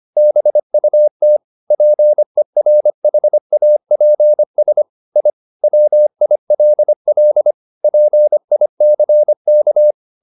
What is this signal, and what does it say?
Morse code, heard as a keyed tone, about 25 wpm, 600 Hz. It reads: BUT PERHAPS I WILL PICK
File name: the old man and the sea part 1 25wpm